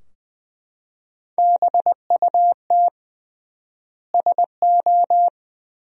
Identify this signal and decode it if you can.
Morse code, heard as a keyed tone, about 21 wpm, 700 Hz. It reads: BUT SO
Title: 3x Word-Spacing - Sentences from Top 100 Words - 20wpm